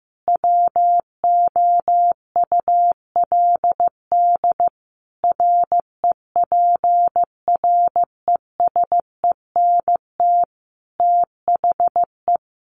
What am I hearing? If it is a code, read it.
Morse code, heard as a keyed tone, about 15 wpm, 700 Hz. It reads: WOULD REPRESENT THE
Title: Sentences from Top 1200 Words 15wpm